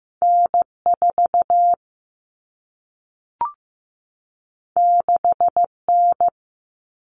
Morse code, heard as a keyed tone, about 15 wpm, 700 Hz. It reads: N4 6N
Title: All Cut Numbers - 2-Digit 15wpm